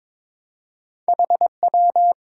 Morse code, heard as a keyed tone, about 22 wpm, 700 Hz. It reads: HW
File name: Top 90 QSO Elements - 4x Character Spacing - 22wpm